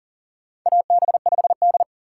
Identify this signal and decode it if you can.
Morse code, heard as a keyed tone, about 40 wpm, 700 Hz. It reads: AB5D
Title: USA Call Signs - 4 Characters - 6x Character Spacing - 40wpm